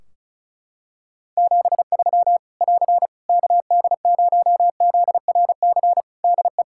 Morse code, heard as a keyed tone, about 34 wpm, 700 Hz. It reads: 73 <AR> KD0ZRC DE
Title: K1USN SST QSO Practice 35wpm